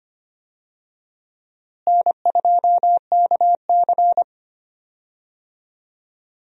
Morse code, heard as a keyed tone, about 25 wpm, 700 Hz. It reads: N2KC
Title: USA Call Signs - 4 Characters - Repeated 3x - 25wpm